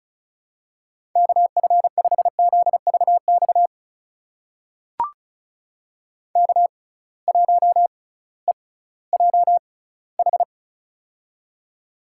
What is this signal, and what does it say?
Morse code, heard as a keyed tone, about 36 wpm, spaced as for 19 wpm, 700 Hz. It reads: KF5ZVX K1EJH
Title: USA Call Signs - 6x Character Spacing - 35wpm